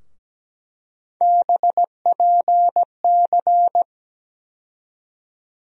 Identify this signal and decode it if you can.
Morse code, heard as a keyed tone, about 17 wpm, 700 Hz. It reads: BPC
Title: USA Call Sign Postfixes - 3 Letters 17wpm